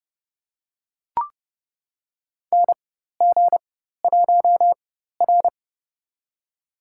Morse code, heard as a keyed tone, about 30 wpm, spaced as for 14 wpm, 700 Hz. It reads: NG1R
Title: USA Call Signs - 4x Character Spacing - 30wpm